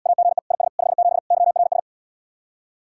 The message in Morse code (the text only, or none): AGE IS 57 YRS